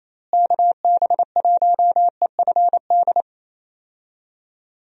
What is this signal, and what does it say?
Morse code, heard as a keyed tone, about 28 wpm, 700 Hz. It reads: KB1EFD
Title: USA Call Signs - 6 Characters - Repeated 2x - 28wpm